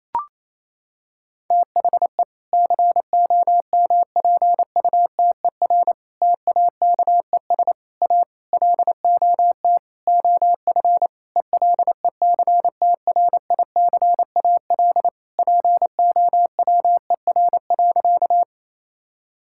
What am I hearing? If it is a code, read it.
Morse code, heard as a keyed tone, about 28 wpm, 700 Hz. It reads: THE COMPUTER TAKES A LOT OF ELECTRICAL POWER.
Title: Sentences from Top 1000 Words - Review Part 6 - 28wpm